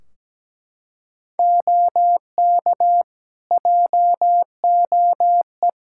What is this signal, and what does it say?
Morse code, heard as a keyed tone, about 17 wpm, 700 Hz. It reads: OK JOE
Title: Practice QSO - Exchange 3 - More Info - 17wpm